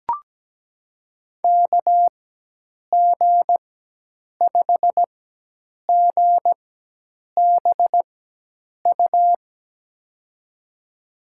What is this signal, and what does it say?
Morse code, heard as a keyed tone, about 17 wpm, 700 Hz. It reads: K G 5 G B U
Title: USA Call Signs - 6 Characters - 4x Character Spacing - 17wpm